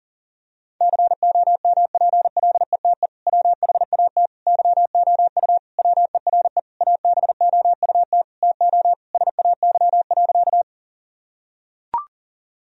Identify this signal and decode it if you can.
Morse code, heard as a keyed tone, about 40 wpm, 700 Hz. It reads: COMPLETE WHAT YOU WERE ABOUT TO SAY.
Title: Sentences from Top 1000 Words - Review Part 2 - 40wpm